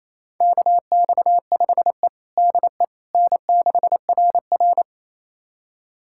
Morse code, heard as a keyed tone, about 27 wpm, 700 Hz. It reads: KX5E DE N6RR
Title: USA Callsign DE Callsign - 4 Characters - 28wpm